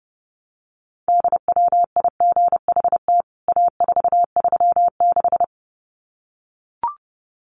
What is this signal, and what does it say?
Morse code, heard as a keyed tone, about 30 wpm, 700 Hz. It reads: DWIGHT A436
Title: CWT Exchange 30wpm